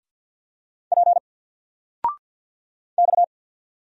Morse code, heard as a keyed tone, about 50 wpm, 700 Hz. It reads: P X
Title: Single Letter-Number 50wpm